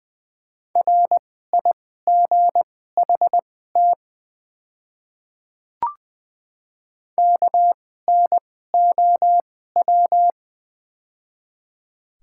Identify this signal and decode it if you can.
Morse code, heard as a keyed tone, about 20 wpm, spaced as for 15 wpm, 690 Hz. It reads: RIGHT KNOW
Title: 2x Character Spacing - Top 200 Words - 20wpm